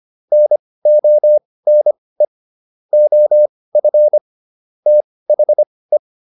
Morse code, heard as a keyed tone, about 25 wpm, spaced as for 18 wpm, 600 Hz. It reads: NONE OF THE